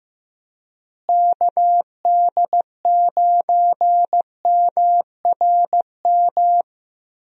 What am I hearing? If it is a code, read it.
Morse code, heard as a keyed tone, about 15 wpm, 700 Hz. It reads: KD9MRM
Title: USA Call Signs - 6 Characters - Repeated 2x - 15wpm